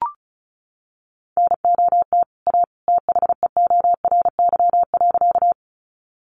Morse code, heard as a keyed tone, about 34 wpm, 700 Hz. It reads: NOT A THEORY.
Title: Sentences from Top 600 Words 35wpm